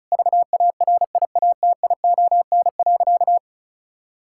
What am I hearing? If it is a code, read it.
Morse code, heard as a keyed tone, about 34 wpm, 700 Hz. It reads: VARIATION.
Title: Sentences from Top 1600 Words 35wpm